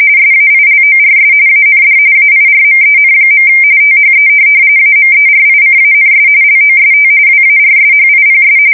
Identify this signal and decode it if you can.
RTTY, about 45.45 baud. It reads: SAILORS AND AIRMEN. THE AMERICANS LOST WARSHIPS
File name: fsk170 / USS-New-Jersey